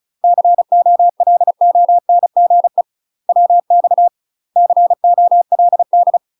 Morse code, heard as a keyed tone, about 36 wpm, 700 Hz. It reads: COLONGE WX COLD
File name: UK MASTER QSOs TRAINING / 35WPM / UK MASTER QSOs 0012